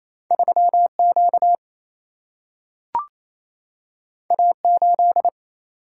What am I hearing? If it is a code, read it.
Morse code, heard as a keyed tone, about 28 wpm, 700 Hz. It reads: 3Q A8
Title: Single-Letter Single-Number 28wpm